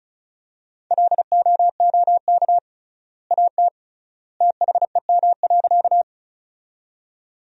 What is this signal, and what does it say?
Morse code, heard as a keyed tone, about 34 wpm, 700 Hz. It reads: LOOK AT THEM.
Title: 3x Word-Spacing - Sentences from Top 100 Words - 35wpm